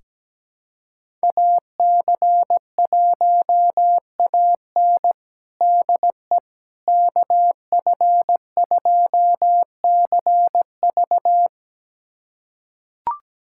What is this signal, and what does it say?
Morse code, heard as a keyed tone, about 17 wpm, 700 Hz. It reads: AC1AN DE KF2CV
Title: USA Callsign DE Callsign - 5 Characters - 17wpm